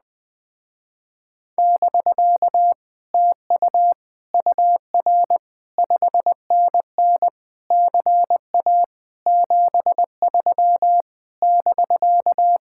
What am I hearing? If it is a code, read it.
Morse code, heard as a keyed tone, about 20 wpm, 700 Hz. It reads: <BK> TU UR 5NN CA 73 <BK>